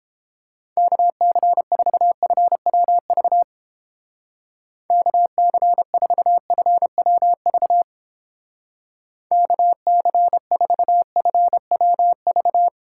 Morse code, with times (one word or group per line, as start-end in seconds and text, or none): 0.8-3.4 KC4FWV
4.9-7.8 KC4FWV
9.3-12.7 KC4FWV